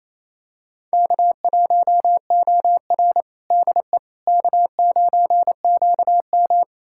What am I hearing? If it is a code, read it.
Morse code, heard as a keyed tone, about 28 wpm, 700 Hz. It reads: K1OR DE K9QM
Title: USA Callsign DE Callsign - 4 Characters - 28wpm